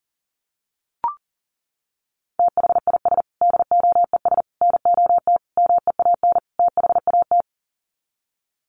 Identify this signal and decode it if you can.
Morse code, heard as a keyed tone, about 40 wpm, 700 Hz. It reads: THIS DOES NOT MEAN THAT
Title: Sets of 5 Words Encore2 40wpm